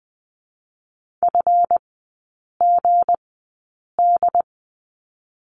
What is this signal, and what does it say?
Morse code, heard as a keyed tone, about 20 wpm, spaced as for 8 wpm, 700 Hz. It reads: FGD
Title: 218 Lesson 35e - Number 0 - Three Characters 20wpm